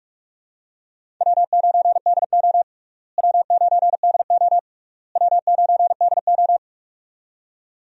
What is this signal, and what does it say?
Morse code, heard as a keyed tone, about 46 wpm, 700 Hz. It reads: W9DO W9DO W9DO
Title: USA Call Signs - 4 Characters - Repeated 3x - 45wpm